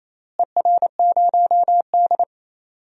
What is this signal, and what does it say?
Morse code, heard as a keyed tone, about 28 wpm, 700 Hz. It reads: ER0D
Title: DXCC-100 Call Signs - 4 Characters - Repeated 3x - 28wpm